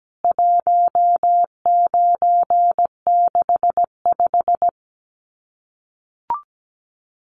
Morse code, heard as a keyed tone, about 17 wpm, 700 Hz. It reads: 1965
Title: CWT Exchange 17wpm